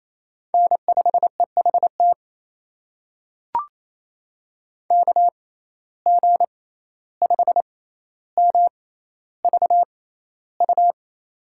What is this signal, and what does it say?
Morse code, heard as a keyed tone, about 28 wpm, spaced as for 14 wpm, 700 Hz. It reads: N5EHT KG5MVU
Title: USA Call Signs - 6x Character Spacing - 28wpm